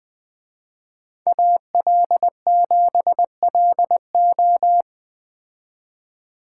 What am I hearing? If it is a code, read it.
Morse code, heard as a keyed tone, about 19 wpm, 700 Hz. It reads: AL7LO